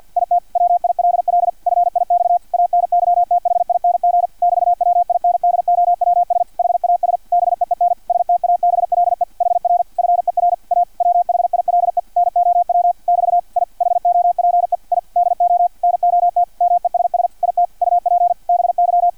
Morse code, heard as a keyed tone, about 50 wpm, 700 Hz, with radio noise. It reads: IT WILL FIX ANYTHING <BT>WINDOWS HAS BEEN STABLE HR FER A WHILE NOW <BT> I HOPE I DO NOT MESS IT UP BY
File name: archived / CRSnet